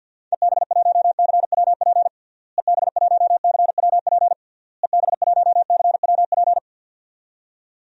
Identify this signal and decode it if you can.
Morse code, heard as a keyed tone, about 52 wpm, 700 Hz. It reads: EB1CWP EB1CWP EB1CWP
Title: DXCC-100 Call Signs - 6 Characters - Repeated 3x - 50wpm